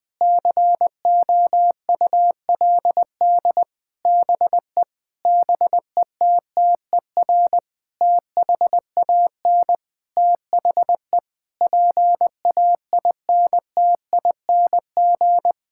Morse code, heard as a keyed tone, about 21 wpm, 700 Hz. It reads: COULD BE BETTER THAN THE PAINTING
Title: Sentences from Top 1000 Words - Review Part 6 - 20wpm